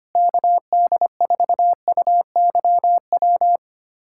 Morse code, heard as a keyed tone, about 25 wpm, 700 Hz. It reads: KD4UYW
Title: USA Callsign DE Callsign - 6 Characters - 25wpm